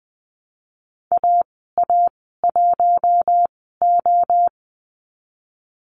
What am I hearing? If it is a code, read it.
Morse code, heard as a keyed tone, about 20 wpm, spaced as for 14 wpm, 700 Hz. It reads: AA1O
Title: 042 Lesson 8g - Number 1 - Call Signs 20wpm